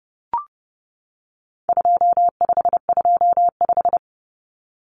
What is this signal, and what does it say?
Morse code, heard as a keyed tone, about 30 wpm, 700 Hz. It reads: 2525
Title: Common Cut Numbers - 4-Digit 30wpm